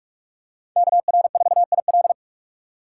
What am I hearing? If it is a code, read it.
Morse code, heard as a keyed tone, about 46 wpm, 700 Hz. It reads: KR4IL
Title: USA Call Signs - 5 Characters - 4x Character Spacing - 45wpm